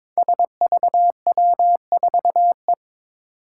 Morse code, heard as a keyed tone, about 22 wpm, 700 Hz. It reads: SVW4E
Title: Five Challenging Letter-Number Combinations 22wpm